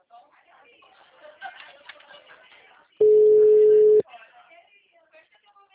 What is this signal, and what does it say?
Ringback tone (Europe, 425 Hz), with speech.